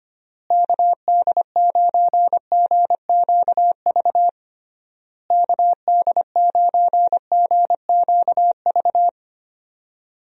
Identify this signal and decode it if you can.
Morse code, heard as a keyed tone, about 25 wpm, 700 Hz. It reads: KD9GQV KD9GQV
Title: USA Call Signs - 6 Characters - Repeated 3x - 25wpm